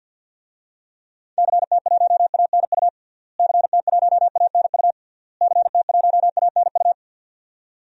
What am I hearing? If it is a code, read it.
Morse code, heard as a keyed tone, about 48 wpm, 700 Hz. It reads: CT1ANU CT1ANU CT1ANU
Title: DXCC-100 Call Signs - 6 Characters - Repeated 3x - 50wpm